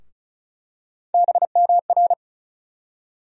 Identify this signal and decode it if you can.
Morse code, heard as a keyed tone, about 36 wpm, 700 Hz. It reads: BMR
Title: USA Call Sign Postfixes - 3 Letters 35wpm